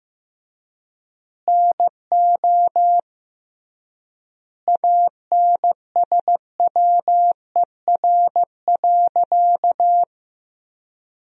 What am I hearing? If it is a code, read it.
Morse code, heard as a keyed tone, about 15 wpm, 700 Hz. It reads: NO ANSWER.
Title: 3x Word-Spacing - Sentences from Top 500 Words - 15wpm